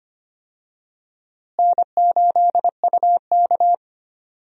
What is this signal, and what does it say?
Morse code, heard as a keyed tone, about 25 wpm, 700 Hz. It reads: N8UK
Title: USA Call Signs - 4 Characters - Repeated 3x - 25wpm